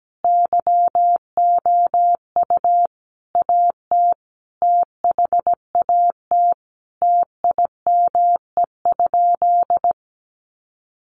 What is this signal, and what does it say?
Morse code, heard as a keyed tone, about 17 wpm, 700 Hz. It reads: YOU AT THAT TIME?
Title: Sentences from Top 200 Words 17wpm